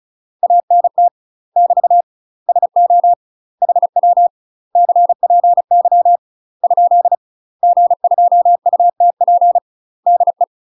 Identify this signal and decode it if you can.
Morse code, heard as a keyed tone, about 36 wpm, 700 Hz. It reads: ANT <BT> SO HW CPY ? G2UTP DE
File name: UK MASTER QSOs TRAINING / 35WPM / UK MASTER QSOs 0022